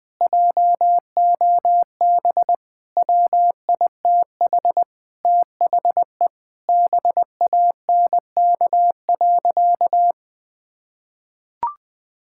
Morse code, heard as a keyed tone, about 20 wpm, 700 Hz. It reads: JOB WITH THE BANK.
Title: Sentences from Top 1000 Words - Review Part 6 - 20wpm